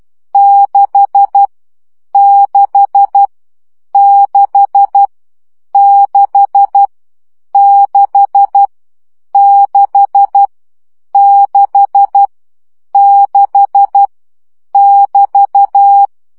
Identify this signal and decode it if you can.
Morse code, heard as a keyed tone, about 12 wpm, 800 Hz. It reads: 6 6 6 6 6 6 6 6 <BT>
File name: CW gyakorlas 6 10 baud